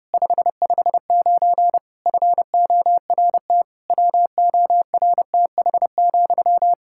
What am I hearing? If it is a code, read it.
Morse code, heard as a keyed tone, about 30 wpm, 700 Hz. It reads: 559 FORT WORTH,
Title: Practice QSO - Exchange 1 - Beginning - 30wpm